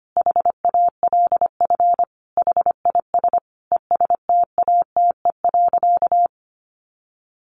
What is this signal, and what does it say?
Morse code, heard as a keyed tone, about 25 wpm, 700 Hz. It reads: HALF HIS ESTATE.
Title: Sentences from Top 1200 Words 25wpm